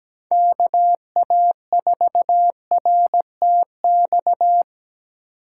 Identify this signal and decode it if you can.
Morse code, heard as a keyed tone, about 20 wpm, 700 Hz. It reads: KA4RTX
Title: Speed Racing - USA 2x3 Call Signs 17wpm